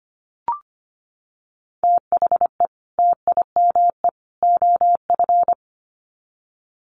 Morse code, heard as a keyed tone, about 25 wpm, 700 Hz. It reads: THE TIME OF
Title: Sets of 3 Words 25wpm